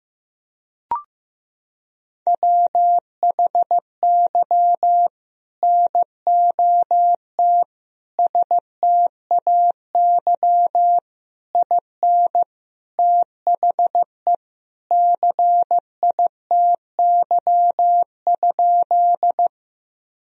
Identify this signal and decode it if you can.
Morse code, heard as a keyed tone, about 15 wpm, 700 Hz. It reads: WHY NOT STAY IN THE CITY?
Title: Sentences from Top 1000 Words - Review Part 1 - 15wpm